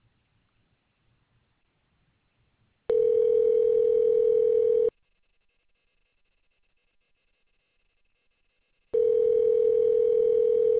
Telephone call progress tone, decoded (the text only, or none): ringback tone